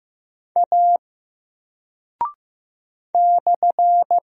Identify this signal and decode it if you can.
Morse code, heard as a keyed tone, about 15 wpm, 700 Hz. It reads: A /
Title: Single Character (Letter:Number:Punctuation:Prosign) 15wpm